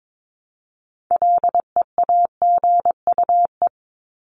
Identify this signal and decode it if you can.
Morse code, heard as a keyed tone, about 22 wpm, 700 Hz. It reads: LEAGUE
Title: Top 1751-2000 Words 22wpm